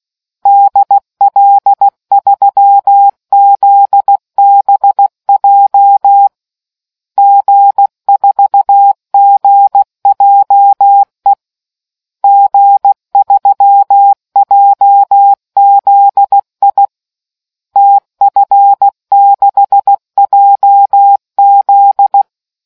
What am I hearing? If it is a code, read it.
Morse code, heard as a keyed tone, about 16 wpm, 800 Hz. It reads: DL3ZBJ G4GJE G3JZI TF6JZ